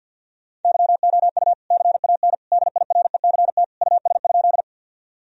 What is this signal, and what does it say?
Morse code, heard as a keyed tone, about 50 wpm, 700 Hz. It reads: YOU CAN DIRECT US?
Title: Sentences from Top 1000 Words - Review Part 2 - 50wpm